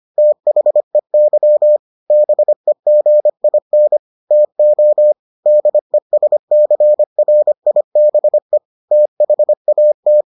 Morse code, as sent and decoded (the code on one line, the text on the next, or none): - .... -.-- / -... --. .. -. / - --- / -.. ... -.-. .-. .. -... / - .... .- -
THEY BEGIN TO DESCRIBE THAT